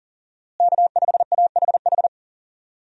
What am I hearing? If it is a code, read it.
Morse code, heard as a keyed tone, about 40 wpm, 690 Hz. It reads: K5AHH
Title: USA Call Signs - 5 Characters - Repeated 3x - 40wpm